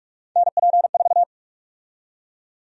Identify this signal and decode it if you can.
Morse code, heard as a keyed tone, about 41 wpm, 700 Hz. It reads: NP4
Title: USA Call Sign Prefixes - Repeated 2x - 45wpm